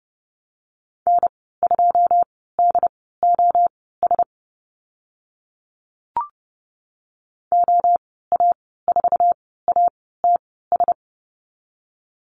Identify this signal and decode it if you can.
Morse code, heard as a keyed tone, about 30 wpm, spaced as for 17 wpm, 700 Hz. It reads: N2DOS OA4ATS
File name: Lesson 15g - Number 5 - Call Signs 30wpm